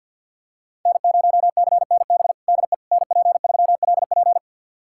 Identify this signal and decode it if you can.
Morse code, heard as a keyed tone, about 47 wpm, 700 Hz. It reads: N0CNB DE NP3LP
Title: USA Callsign DE Callsign - 5 Characters - 50wpm